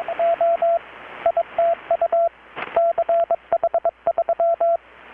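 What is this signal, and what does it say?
Morse code, heard as a keyed tone, about 22 wpm, 680 Hz, with radio noise. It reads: O ITU CH3
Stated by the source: Morse from KPH's Night of Nights XX maritime radiotelegraphy event, received in Chiapas, Mexico about 3,800 km away on a Kenwood R-600 with a grounded 10 m long-wire antenna.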